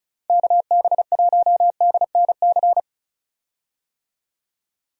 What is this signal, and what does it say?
Morse code, heard as a keyed tone, about 36 wpm, 700 Hz. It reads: KB1DNC
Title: USA Call Signs - 6 Characters - 35wpm